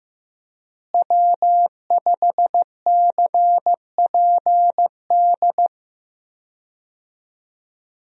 Morse code, heard as a keyed tone, about 15 wpm, 700 Hz. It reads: W5CPD
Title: USA Call Signs - 5 Characters - Repeated 3x - 15wpm